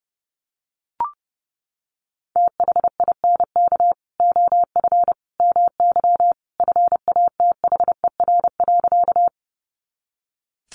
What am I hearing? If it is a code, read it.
Morse code, heard as a keyed tone, about 30 wpm, 700 Hz. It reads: THINK OF MY FATHER.
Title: Sentences from Top 300 Words 30wpm